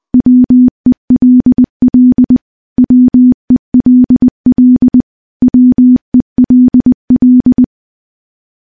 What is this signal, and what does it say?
Morse code, heard as a keyed tone, about 20 wpm, 250 Hz. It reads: WELL WELL WELL
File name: morse / morse-training-20WPM